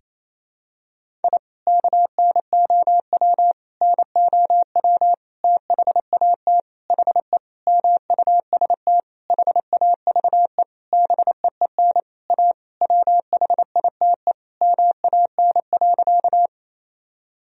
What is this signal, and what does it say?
Morse code, heard as a keyed tone, about 28 wpm, 700 Hz. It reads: I KNOW NOW THAT HE MUST HAVE BEEN A WHITE MAN.
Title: Sentences from Top 1000 Words - Review Part 1 - 28wpm